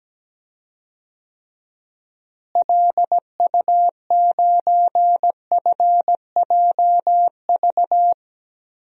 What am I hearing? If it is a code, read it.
Morse code, heard as a keyed tone, about 17 wpm, 700 Hz. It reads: LU9FJV
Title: DXCC-100 Call Signs - 6 Characters - Repeated 3x - 17wpm